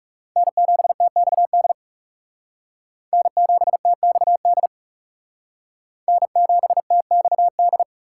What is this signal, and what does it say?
Morse code, heard as a keyed tone, about 40 wpm, 700 Hz. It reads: N7TXD N7TXD N7TXD